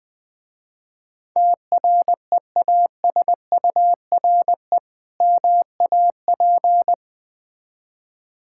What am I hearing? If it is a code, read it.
Morse code, heard as a keyed tone, about 21 wpm, 700 Hz. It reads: TREASURE MAP
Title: Difficult Words with Hints - 20wpm